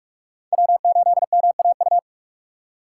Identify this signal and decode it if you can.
Morse code, heard as a keyed tone, about 44 wpm, 700 Hz. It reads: W8MAU
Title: USA Call Signs - 5 Characters - 4x Character Spacing - 45wpm